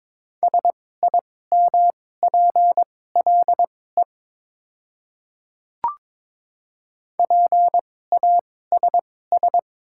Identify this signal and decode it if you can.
Morse code, heard as a keyed tone, about 22 wpm, spaced as for 16 wpm, 700 Hz. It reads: SIMPLE PASS
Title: 2x Character Spacing - Top 500 Words - 22wpm